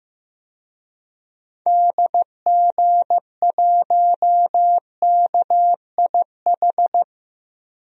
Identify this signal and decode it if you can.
Morse code, heard as a keyed tone, about 15 wpm, 700 Hz. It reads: DG1KIH
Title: DXCC-100 Call Signs - 6 Characters - Repeated 3x - 15wpm